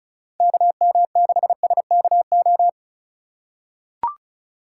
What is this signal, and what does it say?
Morse code, heard as a keyed tone, about 36 wpm, 700 Hz. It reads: KM6SKO
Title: USA Call Signs - 2x Character Spacing - 35wpm